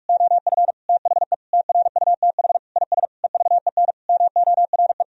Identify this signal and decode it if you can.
Morse code, heard as a keyed tone, about 46 wpm, 700 Hz. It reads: OF THE TRUTH IS EVEN MORE
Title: Sentences from Top 1000 Words - Review Part 3 - 45wpm